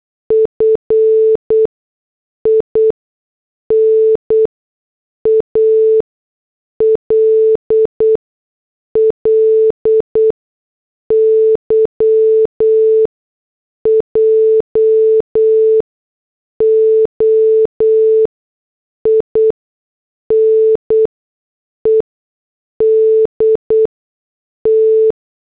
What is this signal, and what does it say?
Morse code, heard as a keyed tone, about 8 wpm, spaced as for 6 wpm, 440 Hz. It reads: FINALLY JOINED T